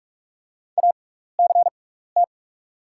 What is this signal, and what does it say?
Morse code, heard as a keyed tone, about 44 wpm, spaced as for 15 wpm, 700 Hz. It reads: ACT
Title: 6x Character Spacing - Top 200 Words - 45wpm